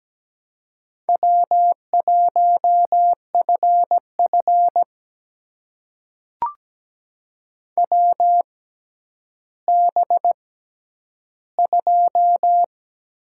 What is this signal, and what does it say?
Morse code, heard as a keyed tone, about 17 wpm, spaced as for 10 wpm, 700 Hz. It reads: W1FF WB2
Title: USA Call Signs - 6x Character Spacing - 17wpm